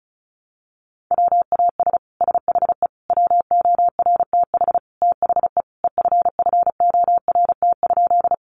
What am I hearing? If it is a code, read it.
Morse code, heard as a keyed tone, about 34 wpm, 700 Hz. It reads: WAS SHE WORTH THE EFFORT?
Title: Sentences from Top 1200 Words 35wpm